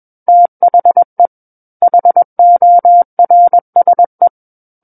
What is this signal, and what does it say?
Morse code, heard as a keyed tone, about 21 wpm, 700 Hz. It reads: THE HORSE
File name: Book One / 12-Chapter 12 - 21 WPM